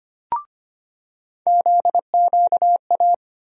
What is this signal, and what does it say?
Morse code, heard as a keyed tone, about 25 wpm, 700 Hz. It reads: ZQA
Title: USA Call Sign Postfixes - 3 Letters 25wpm